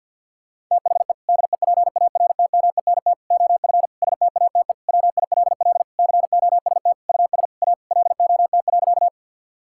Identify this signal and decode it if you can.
Morse code, heard as a keyed tone, about 52 wpm, 700 Hz. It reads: THE DEPARTMENT OF STATE WILL COST US A LOT.
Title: Sentences from Top 1000 Words - Review Part 5 - 50wpm